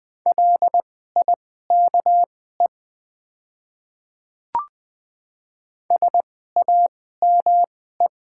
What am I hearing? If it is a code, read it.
Morse code, heard as a keyed tone, about 20 wpm, spaced as for 14 wpm, 700 Hz. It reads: LIKE SAME
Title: 2x Character Spacing - Top 200 Words - 20wpm